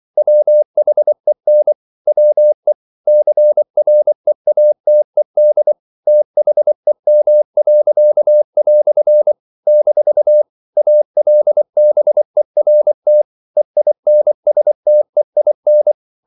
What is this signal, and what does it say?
Morse code, heard as a keyed tone, about 24 wpm, 600 Hz. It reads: WHEN WE CREATED THEM." - ALBERT EINSTEIN
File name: CW / Practice / EnglishQuotations0224